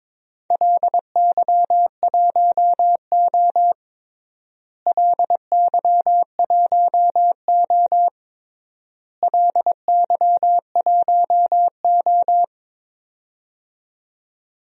Morse code, heard as a keyed tone, about 22 wpm, 700 Hz. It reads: LY1O LY1O LY1O
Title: DXCC-100 Call Signs - 4 Characters - Repeated 3x - 22wpm